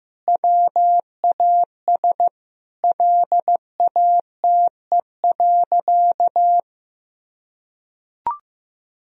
Morse code, heard as a keyed tone, about 15 wpm, 700 Hz. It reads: WAS LATE.
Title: Sentences from Top 1000 Words - Review Part 3 - 15wpm